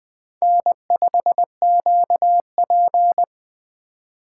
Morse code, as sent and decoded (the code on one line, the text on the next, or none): -. ..... --.- .--.
N5QP